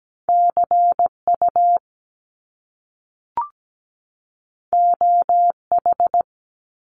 Morse code, heard as a keyed tone, about 17 wpm, 700 Hz. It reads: CU OH
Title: Top Two Letter Combinations in Words 17wpm